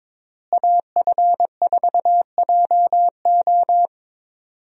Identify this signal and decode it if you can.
Morse code, heard as a keyed tone, about 22 wpm, 700 Hz. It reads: AF4JO
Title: USA Call Signs - 5 Characters - Repeated 3x - 22wpm